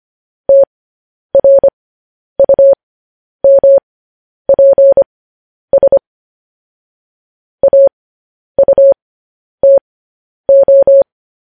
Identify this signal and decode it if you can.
Morse code, heard as a keyed tone, about 25 wpm, spaced as for 10 wpm, 550 Hz. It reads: TRUMPS AUTO
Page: news-headlines-in-morse-code-at-10-wpm